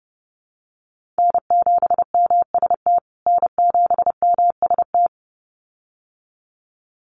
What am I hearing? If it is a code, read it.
Morse code, heard as a keyed tone, about 30 wpm, 700 Hz. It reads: N7MST N7MST
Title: USA Call Signs Repeated 2x 30wpm